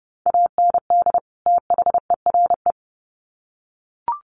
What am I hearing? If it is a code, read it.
Morse code, heard as a keyed tone, about 30 wpm, 700 Hz. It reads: AND THERE
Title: Sets of 2 Words 30wpm